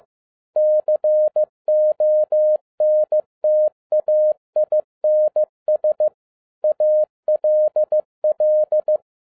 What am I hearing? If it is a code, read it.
Morse code, heard as a keyed tone, about 15 wpm, 610 Hz. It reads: CONTAINS ALL
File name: 2012-01-mp3 / n17-SUMMARY OF FEA NET IN 2011-15wpm